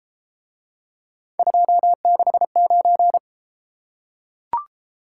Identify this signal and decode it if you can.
Morse code, heard as a keyed tone, about 32 wpm, 700 Hz. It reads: 269